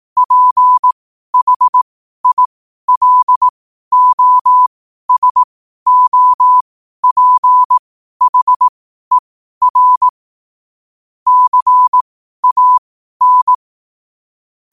Morse code, heard as a keyed tone, about 18 wpm, spaced as for 12 wpm, 1000 Hz. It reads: PHILOSOPHER CAN
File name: qotdfiles / 12wpm